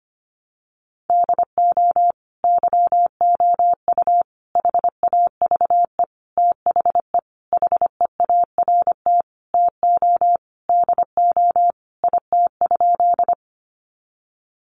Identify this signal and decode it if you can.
Morse code, heard as a keyed tone, about 25 wpm, 700 Hz. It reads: DO YOU HAVE THE HEART TO DO IT?
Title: Sentences from Top 1200 Words 25wpm